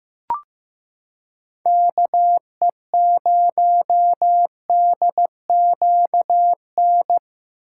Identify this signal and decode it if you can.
Morse code, heard as a keyed tone, about 15 wpm, 700 Hz. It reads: KE0DQN
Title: USA Callsign DE Callsign - 6 Characters - 15wpm